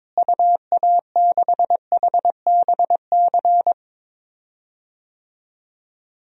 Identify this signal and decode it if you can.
Morse code, heard as a keyed tone, about 21 wpm, 700 Hz. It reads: UA6HBC
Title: DXCC-100 Call Signs - 6 Characters - Repeated 2x - 22wpm